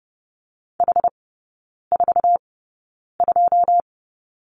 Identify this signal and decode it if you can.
Morse code, heard as a keyed tone, about 30 wpm, spaced as for 9 wpm, 700 Hz. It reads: H42
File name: Lesson 19e - Letter w - Three Characters 30wpm